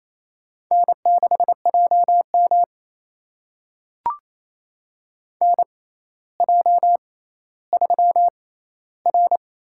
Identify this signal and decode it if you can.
Morse code, heard as a keyed tone, about 27 wpm, spaced as for 14 wpm, 700 Hz. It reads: N6JM NJ3R
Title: USA Call Signs - 4 Characters - 6x Character Spacing - 28wpm